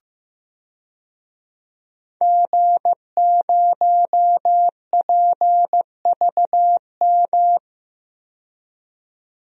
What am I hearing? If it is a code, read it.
Morse code, heard as a keyed tone, about 16 wpm, 700 Hz. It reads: G0PVM
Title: DXCC-100 Call Signs - 5 Characters - 15wpm